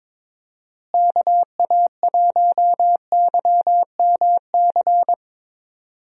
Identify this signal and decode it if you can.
Morse code, heard as a keyed tone, about 22 wpm, 700 Hz. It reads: KA1YMC
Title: USA Call Signs - 6 Characters - Repeated 3x - 22wpm